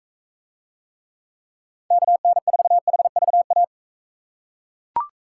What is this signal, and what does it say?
Morse code, heard as a keyed tone, about 42 wpm, 700 Hz. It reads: KN4HVA